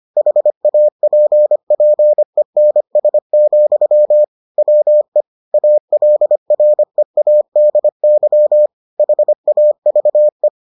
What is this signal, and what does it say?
Morse code, heard as a keyed tone, about 25 wpm, 600 Hz. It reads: HAPPENS, WE ALREADY HAVE